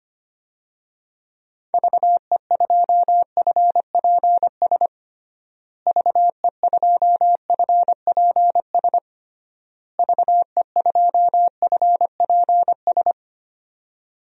Morse code, heard as a keyed tone, about 25 wpm, 700 Hz. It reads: VE2FPS VE2FPS VE2FPS
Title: DXCC-100 Call Signs - 6 Characters - Repeated 3x - 25wpm